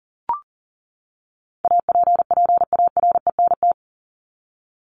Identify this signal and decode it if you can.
Morse code, heard as a keyed tone, about 40 wpm, 700 Hz. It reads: APPARENT
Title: LY Suffix - Top 100 Words with Base Words 40wpm